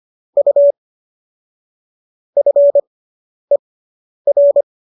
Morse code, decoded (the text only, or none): U FER